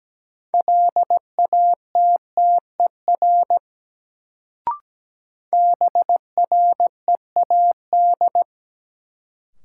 Morse code, heard as a keyed tone, about 17 wpm, 700 Hz. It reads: LATTER BREAD
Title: Sentences from Top 1600 Words - New Word Drill - 17wpm